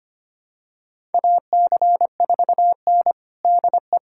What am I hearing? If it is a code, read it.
Morse code, heard as a keyed tone, about 25 wpm, 700 Hz. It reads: AC4N DE
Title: USA Callsign DE Callsign - 4 Characters - 25wpm